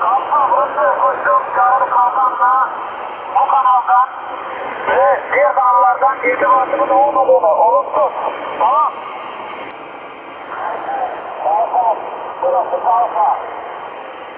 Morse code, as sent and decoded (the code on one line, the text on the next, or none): none